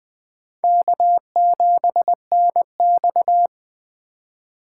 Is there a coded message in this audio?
Yes: Morse code from 0.6 to 3.5 s, heard as a keyed tone, about 22 wpm, 700 Hz. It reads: K7NX